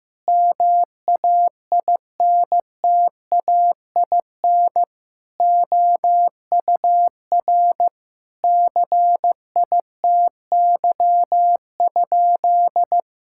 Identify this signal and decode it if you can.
Morse code, heard as a keyed tone, about 15 wpm, 700 Hz. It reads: MAINTAIN OUR CITY?
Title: Sentences from Top 1000 Words - Review Part 5 - 15wpm